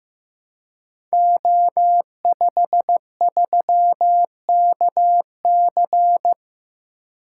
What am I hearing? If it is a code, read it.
Morse code, heard as a keyed tone, about 15 wpm, 700 Hz. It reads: O53KC
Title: Code Groups - 5 Letter:Numbers - 15wpm